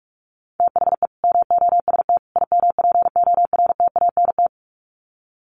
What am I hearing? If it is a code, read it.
Morse code, heard as a keyed tone, about 44 wpm, 700 Hz. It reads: THE MOST IMPORTANT
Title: Sets of 3 Words 45wpm